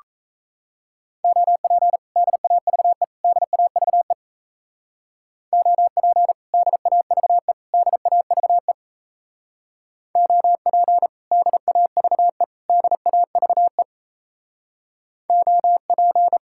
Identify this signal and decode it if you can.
Morse code, heard as a keyed tone, about 33 wpm, 700 Hz. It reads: OP DAVE DAVE OP DAVE DAVE OP DAVE DAVE OP
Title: Speed-Racing Common QSO Phrases Encore - 28wpm